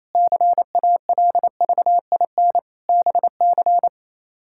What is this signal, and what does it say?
Morse code, heard as a keyed tone, about 28 wpm, 700 Hz. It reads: CALVIN BC
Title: K1USN SST QSO Practice - 28wpm